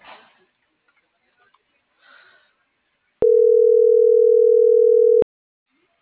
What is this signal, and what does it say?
Ringback tone (North America), 440 + 480 Hz.
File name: CENTRAL NMP / out patricia.martineznmp - .38957